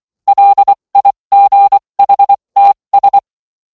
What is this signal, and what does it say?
Morse code, heard as a keyed tone, about 24 wpm, spaced as for 20 wpm, 800 Hz. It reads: LIGHTS